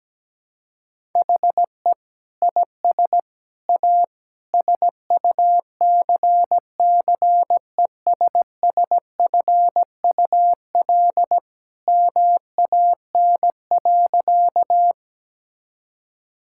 Morse code, heard as a keyed tone, about 17 wpm, 700 Hz. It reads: HE IS A SUCCESSFUL MAN.
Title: Sentences from Top 1000 Words - Review Part 6 - 17wpm